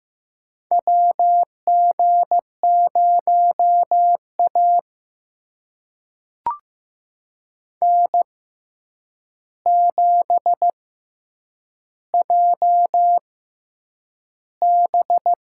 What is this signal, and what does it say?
Morse code, heard as a keyed tone, about 15 wpm, spaced as for 8 wpm, 700 Hz. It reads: WG0A N7JB
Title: USA Call Signs - 4 Characters - 6x Character Spacing - 15wpm